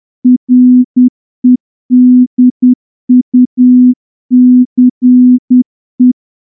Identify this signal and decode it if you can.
Morse code, heard as a keyed tone, about 10 wpm, 250 Hz. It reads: REDUCE
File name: morse-training-10WPM